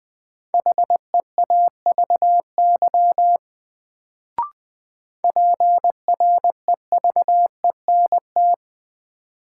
Morse code, heard as a keyed tone, about 20 wpm, 700 Hz. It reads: HEAVY PREVENT
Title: Sentences from Top 1100 Words - New Word Drill Alt - 20wpm